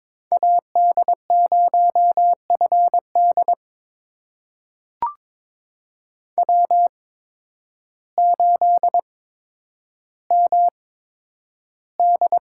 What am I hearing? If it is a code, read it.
Morse code, heard as a keyed tone, about 22 wpm, spaced as for 10 wpm, 700 Hz. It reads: AD0FD W8MD
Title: USA Call Signs - 5 Characters - 8x Character Spacing - 22wpm